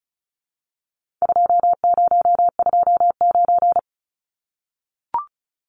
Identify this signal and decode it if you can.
Morse code, heard as a keyed tone, about 34 wpm, 700 Hz. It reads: 2029
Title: Numbers - 4-Digit 35wpm